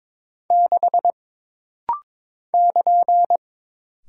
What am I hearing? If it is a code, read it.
Morse code, heard as a keyed tone, about 22 wpm, 700 Hz. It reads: 6 <KN>
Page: This rapid-fire timing is geared to push your ICR (Instant Character Recognition) skills.